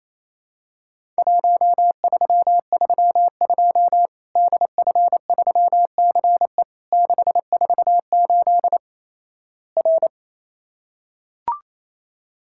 Morse code, heard as a keyed tone, about 28 wpm, 700 Hz. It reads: 1332 DF3CE 648 R
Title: Worked All Europe DX Contest - QTC Exchange - 28wpm